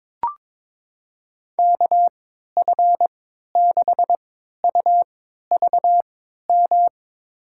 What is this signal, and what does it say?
Morse code, heard as a keyed tone, about 22 wpm, spaced as for 13 wpm, 700 Hz. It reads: KF6UVM
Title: USA Call Signs - 6 Characters - 3x Character Spacing - 22wpm